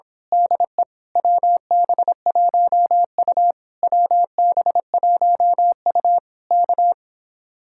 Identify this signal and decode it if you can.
Morse code, heard as a keyed tone, about 26 wpm, 700 Hz. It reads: DE WB1U WB1U K
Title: Speed-Racing Common QSO Phrases - 22wpm